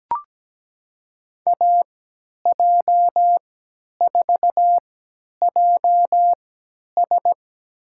Morse code, heard as a keyed tone, about 17 wpm, spaced as for 10 wpm, 700 Hz. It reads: AJ4JS
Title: USA Call Signs - 5 Characters - 3x Character Spacing - 17wpm